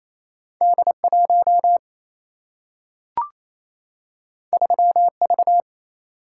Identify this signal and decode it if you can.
Morse code, heard as a keyed tone, about 28 wpm, 700 Hz. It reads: D1 3V
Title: Single-Letter Single-Number 28wpm